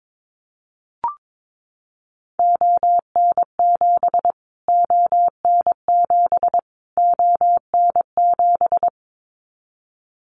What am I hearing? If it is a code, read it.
Morse code, heard as a keyed tone, about 22 wpm, 700 Hz. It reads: ON7 ON7 ON7
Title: Callsign Prefixes for Top 25 DXCC - Repeated 3x - 22wpm